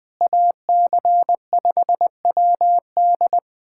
Morse code, heard as a keyed tone, about 20 wpm, 700 Hz. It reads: AC5WD
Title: USA Callsign DE Callsign - 5 Characters - 20wpm